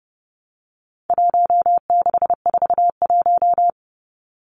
Morse code, heard as a keyed tone, about 30 wpm, 700 Hz. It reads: 1641
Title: Numbers - Contest Style - 4-Digit 30wpm